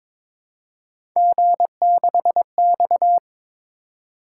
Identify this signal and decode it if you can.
Morse code, heard as a keyed tone, about 22 wpm, 700 Hz. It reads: G6X
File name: practice / 22wpm / Three Letter-Number Combinations - 22wpm